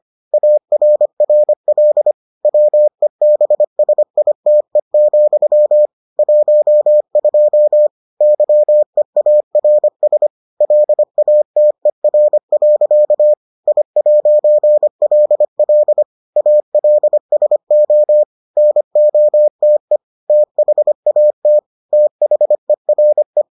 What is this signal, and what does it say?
Morse code, heard as a keyed tone, about 25 wpm, 600 Hz. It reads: ARRL WEBSITE, 12 YEARS LATER. I'LL ALSO NOTE THAT THERE